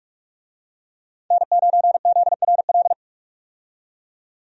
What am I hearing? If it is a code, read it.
Morse code, heard as a keyed tone, about 43 wpm, 700 Hz. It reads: N9ZRL
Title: USA Call Signs - 5 Characters - 45wpm